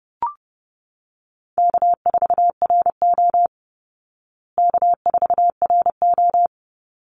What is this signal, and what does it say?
Morse code, heard as a keyed tone, about 30 wpm, 700 Hz. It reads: K4RO K4RO
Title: Easy Call Signs Repeated 2x - 30wpm